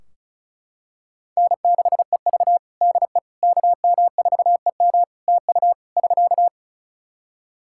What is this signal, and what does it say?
Morse code, heard as a keyed tone, about 36 wpm, 700 Hz. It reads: N6EV DE KM4EM TU <SK>
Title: Practice QSO - Exchange 4 - Ending - 35wpm